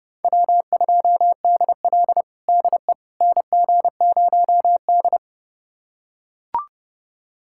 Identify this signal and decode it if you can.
Morse code, heard as a keyed tone, about 30 wpm, 700 Hz. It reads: W2DL DE NG0D
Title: USA Callsign DE Callsign - 4 Characters - 30wpm